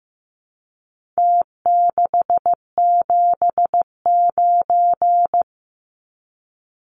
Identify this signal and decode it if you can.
Morse code, heard as a keyed tone, about 15 wpm, 700 Hz. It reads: T679
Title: Numbers - Contest Style - 4-Digit 15wpm